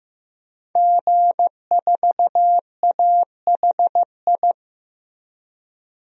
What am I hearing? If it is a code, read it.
Morse code, heard as a keyed tone, about 15 wpm, 700 Hz. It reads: G4AHI
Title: DXCC-100 Call Signs - 5 Characters - 15wpm